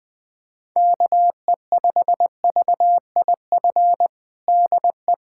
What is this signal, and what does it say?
Morse code, heard as a keyed tone, about 20 wpm, 700 Hz. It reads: KE5VIF DE
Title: USA Callsign DE Callsign - 6 Characters - 20wpm